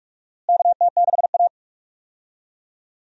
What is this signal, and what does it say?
Morse code, heard as a keyed tone, about 42 wpm, 700 Hz. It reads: KT6A
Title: USA Call Signs - 4 Characters - Repeated 2x - 45wpm